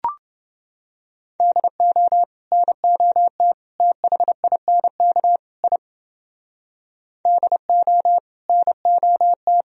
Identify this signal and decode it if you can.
Morse code, heard as a keyed tone, about 29 wpm, 700 Hz. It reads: DO NOT THINK I DO NOT